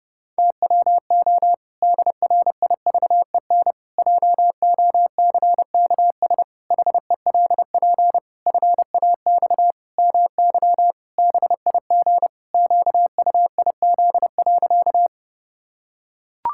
Morse code, heard as a keyed tone, about 30 wpm, 700 Hz. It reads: TWO DRIVEN JOCKS HELP FAX MY BIG QUIZ.
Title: Pangram Challenge v2 - 30wpm